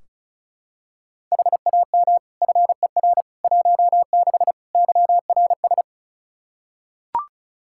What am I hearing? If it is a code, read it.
Morse code, heard as a keyed tone, about 34 wpm, 700 Hz. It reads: HAM FER 16 YRS
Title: K1USN SST QSO Practice 35wpm